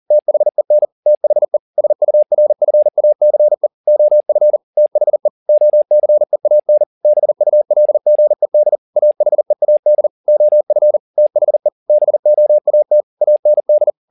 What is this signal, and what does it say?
Morse code, heard as a keyed tone, about 40 wpm, 600 Hz. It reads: THEN THE SURFACE OF THE OCEAN BULGED AHEAD OF THE BOAT AND